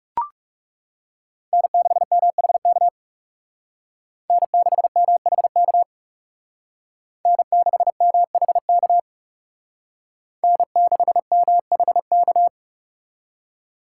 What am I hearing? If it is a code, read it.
Morse code, heard as a keyed tone, about 36 wpm, 700 Hz. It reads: N6MHK N6MHK N6MHK N6MHK
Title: Speed Racing - USA 1x3 Call Signs 30wpm